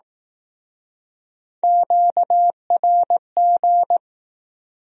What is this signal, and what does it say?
Morse code, heard as a keyed tone, about 19 wpm, 700 Hz. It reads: QRG